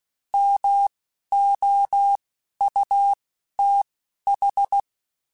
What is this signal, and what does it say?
Morse code, heard as a keyed tone, about 16 wpm, spaced as for 11 wpm, 800 Hz. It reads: MOUTH